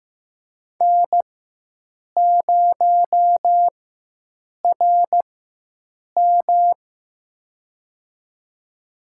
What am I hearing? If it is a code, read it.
Morse code, heard as a keyed tone, about 15 wpm, spaced as for 7 wpm, 700 Hz. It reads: N0RM
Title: USA Call Signs - 4 Characters - 4x Character Spacing - 15wpm